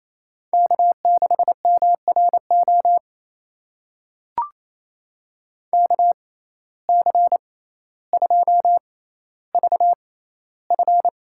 Morse code, heard as a keyed tone, about 28 wpm, spaced as for 14 wpm, 700 Hz. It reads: K6MRO KC2VF
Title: USA Call Signs - 6x Character Spacing - 28wpm